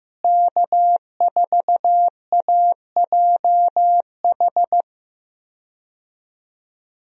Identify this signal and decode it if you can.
Morse code, heard as a keyed tone, about 15 wpm, 700 Hz. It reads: K4AJH
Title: USA Call Signs - 5 Characters - Repeated 2x - 15wpm